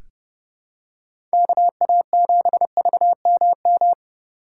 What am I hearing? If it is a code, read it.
Morse code, heard as a keyed tone, about 30 wpm, 700 Hz. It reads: KA7VMM
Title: USA Call Signs - 3x Character Spacing - 30wpm